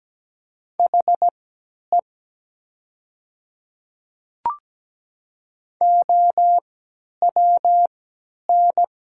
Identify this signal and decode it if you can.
Morse code, heard as a keyed tone, about 17 wpm, spaced as for 10 wpm, 700 Hz. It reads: HE OWN
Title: 3x Character Spacing - Top 500 Words - 17wpm